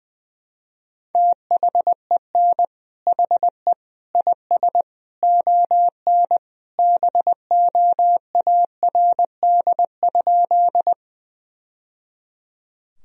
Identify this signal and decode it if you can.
Morse code, heard as a keyed tone, about 20 wpm, 700 Hz. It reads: THEN HE IS ON BOARD?
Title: Sentences from Top 1000 Words - Review Part 4 - 20wpm